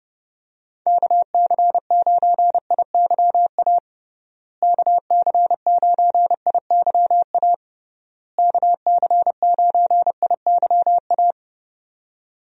Morse code, heard as a keyed tone, about 30 wpm, 700 Hz. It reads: KC9IYA KC9IYA KC9IYA
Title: USA Call Signs - 6 Characters - Repeated 3x - 30wpm